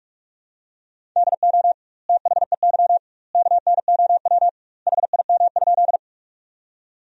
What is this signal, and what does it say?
Morse code, heard as a keyed tone, about 46 wpm, 700 Hz. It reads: DO THEY KNOW HIM?
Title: 2x Word-Spacing - Sentences from Top 100 Words - 45wpm